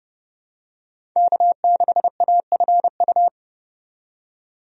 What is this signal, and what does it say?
Morse code, heard as a keyed tone, about 30 wpm, 700 Hz. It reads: K6AFU
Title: USA Call Signs - 5 Characters - 30wpm